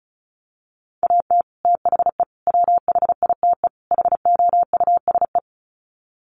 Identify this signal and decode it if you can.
Morse code, heard as a keyed tone, about 34 wpm, 700 Hz. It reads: AT THE WHITE HOUSE
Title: Sets of 4 Words 35wpm